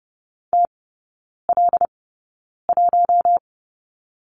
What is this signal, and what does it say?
Morse code, heard as a keyed tone, about 30 wpm, spaced as for 9 wpm, 700 Hz. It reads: TL1
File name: course / 30:10wpm / 087 Lesson 15e - Number 5 - Three Characters 30wpm